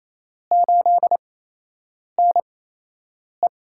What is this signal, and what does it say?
Morse code, heard as a keyed tone, about 28 wpm, spaced as for 8 wpm, 700 Hz. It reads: 8NE
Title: USA Call Signs - 4 Characters - 8x Character Spacing - 28wpm